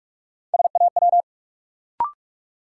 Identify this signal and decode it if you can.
Morse code, heard as a keyed tone, about 43 wpm, 700 Hz. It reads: SAW
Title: Sentences from Top 300 Words - New Word Drill - 45wpm